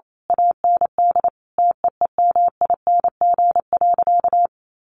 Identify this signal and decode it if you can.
Morse code, heard as a keyed tone, about 28 wpm, 700 Hz. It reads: AND TEEMING.
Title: A Sound Of Thunder 28wpm